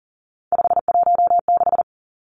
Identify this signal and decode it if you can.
Morse code, heard as a keyed tone, about 41 wpm, 700 Hz. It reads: 516
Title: Three Challenging Letter-Number Combinations 40wpm